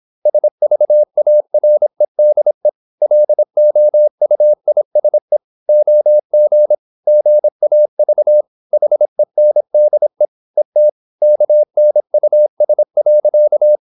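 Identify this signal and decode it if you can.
Morse code, heard as a keyed tone, about 26 wpm, 600 Hz. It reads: SVAREDE LOUISE OG GAV HENDE ET KNUS.
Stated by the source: High-speed (QRQ) Morse at 26 wpm.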